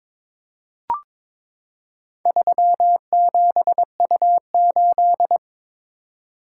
Morse code, heard as a keyed tone, about 22 wpm, 700 Hz. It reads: 37U8
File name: practice / 22wpm / Four Letter-Number Combinations - 22wpm